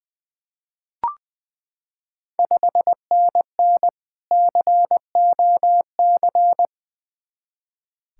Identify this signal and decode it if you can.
Morse code, heard as a keyed tone, about 20 wpm, 700 Hz. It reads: 5NN COC
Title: Ontario QSO Party Exchange 20wpm